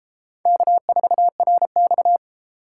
Morse code, heard as a keyed tone, about 29 wpm, 700 Hz. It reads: K4RX